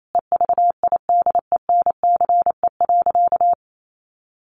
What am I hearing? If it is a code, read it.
Morse code, heard as a keyed tone, about 28 wpm, 700 Hz. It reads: EVIDENCE.
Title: Sentences from Top 1300 Words 28wpm